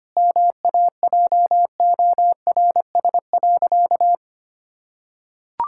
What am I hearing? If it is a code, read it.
Morse code, heard as a keyed tone, about 25 wpm, 700 Hz. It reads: MAJORS.
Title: Pangram Challenge v2 - 25wpm